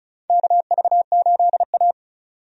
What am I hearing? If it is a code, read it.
Morse code, heard as a keyed tone, about 36 wpm, 700 Hz. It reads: KV8A
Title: USA Call Signs - 4 Characters - 35wpm